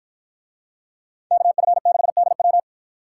Morse code, heard as a keyed tone, about 53 wpm, 700 Hz. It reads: KF6DW